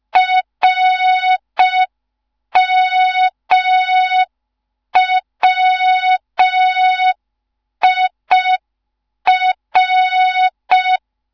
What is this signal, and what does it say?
Morse code, heard as a keyed tone, about 5 wpm, 3700 Hz. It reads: RMWIR
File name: Morze-HA2MN / hang / 15 GYAKORLAT